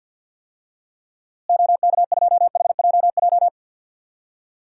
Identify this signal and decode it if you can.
Morse code, heard as a keyed tone, about 51 wpm, 700 Hz. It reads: OK2HJJ
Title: DXCC-100 Call Signs - 6 Characters - Repeated 2x - 50wpm